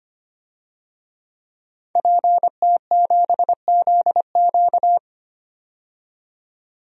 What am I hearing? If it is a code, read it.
Morse code, heard as a keyed tone, about 25 wpm, 700 Hz. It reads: PT7ZQ
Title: DXCC-100 Call Signs - 5 Characters - 25wpm